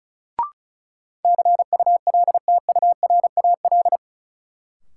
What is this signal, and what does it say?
Morse code, heard as a keyed tone, about 36 wpm, 700 Hz. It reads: CULTURAL
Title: Sentences from Top 1500 Words - New Word Drill - 35wpm